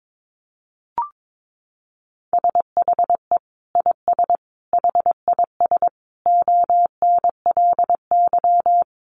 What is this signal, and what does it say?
Morse code, heard as a keyed tone, about 22 wpm, 700 Hz. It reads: SHE IS HIS ONLY
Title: Sentences from Top 1500 Words 22wpm